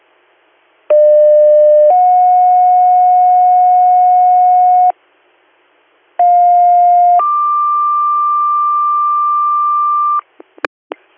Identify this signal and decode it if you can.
Two-tone page, heard as a one-second tone, then a three-second tone, about 600 then 730 Hz, about 710 then 1180 Hz. It is heard from 0.9 to 10.2 s.